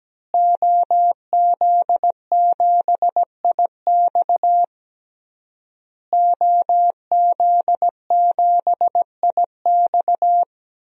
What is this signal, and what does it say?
Morse code, heard as a keyed tone, about 17 wpm, 700 Hz. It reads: OZ7IX OZ7IX
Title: DXCC-100 Call Signs - 5 Characters - Repeated 3x - 17wpm